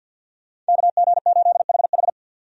Morse code, heard as a keyed tone, about 51 wpm, 700 Hz. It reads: KG8HH
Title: USA Call Signs - 5 Characters - 50wpm